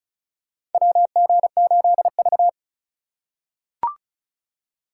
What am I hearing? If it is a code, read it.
Morse code, heard as a keyed tone, about 36 wpm, 700 Hz. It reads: WG8V
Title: USA Call Signs - 2x Character Spacing - 35wpm